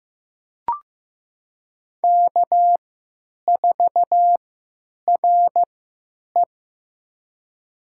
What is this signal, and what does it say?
Morse code, heard as a keyed tone, about 15 wpm, spaced as for 8 wpm, 700 Hz. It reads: K4RE
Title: USA Call Signs - 4 Characters - 3x Character Spacing - 15wpm